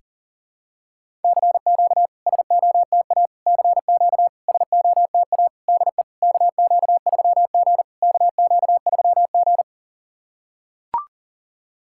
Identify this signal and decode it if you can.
Morse code, heard as a keyed tone, about 40 wpm, 700 Hz. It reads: CQ SOTA CQ SOTA DE KQ3G KQ3G
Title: SOTA QSO Practice 40wpm